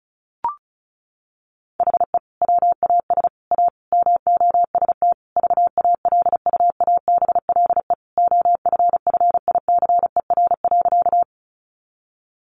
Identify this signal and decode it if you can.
Morse code, heard as a keyed tone, about 34 wpm, 700 Hz. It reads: HE WAS A MOST VALUABLE OFFICER.
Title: Sentences from Top 1200 Words 35wpm